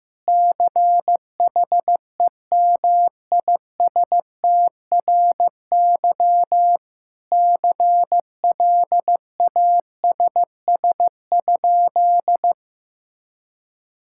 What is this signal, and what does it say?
Morse code, heard as a keyed tone, about 15 wpm, 700 Hz. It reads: CHEMISTRY CLASS?
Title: Sentences from Top 1000 Words - Review Part 6 - 15wpm